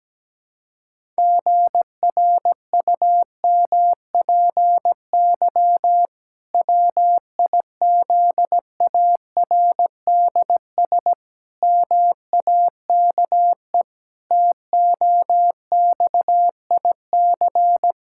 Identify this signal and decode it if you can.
Morse code, heard as a keyed tone, about 17 wpm, 700 Hz. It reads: GRUMPY WIZARDS MAKE TOXIC
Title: Pangram Challenge v2 - 17wpm